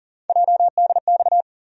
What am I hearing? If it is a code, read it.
Morse code, heard as a keyed tone, about 40 wpm, 700 Hz. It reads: JDX
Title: USA Call Sign Postfixes - 3 Letters 40wpm